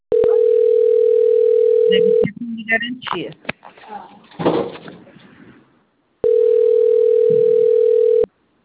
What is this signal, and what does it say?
Ringback tone (North America), 440 + 480 Hz, with speech.
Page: Bebedouro - SP